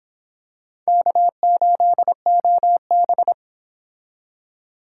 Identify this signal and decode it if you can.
Morse code, heard as a keyed tone, about 25 wpm, 700 Hz. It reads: K8OB